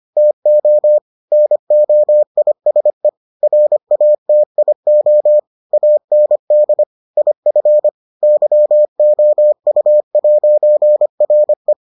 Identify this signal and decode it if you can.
Morse code, heard as a keyed tone, about 25 wpm, 600 Hz. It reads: TO NOISE RATIO AND IF YOU'RE